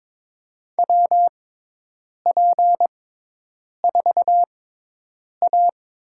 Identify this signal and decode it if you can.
Morse code, heard as a keyed tone, about 22 wpm, spaced as for 8 wpm, 700 Hz. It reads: WP4A
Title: USA Call Signs - 5 Characters - 6x Character Spacing - 22wpm